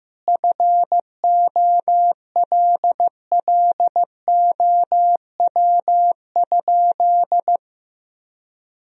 Morse code, heard as a keyed tone, about 15 wpm, 700 Hz. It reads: FOLLOW?
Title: Sentences from Top 1000 Words - Review Part 2 - 15wpm